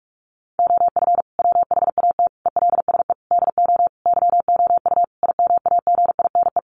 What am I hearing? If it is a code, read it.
Morse code, heard as a keyed tone, about 44 wpm, 700 Hz. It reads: OF WHAT ELSE DO YOU IMAGINE
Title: Sentences from Top 1400 Words 45wpm